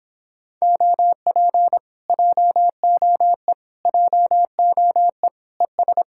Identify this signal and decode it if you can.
Morse code, heard as a keyed tone, about 26 wpm, 700 Hz. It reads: OP JOE JOE ES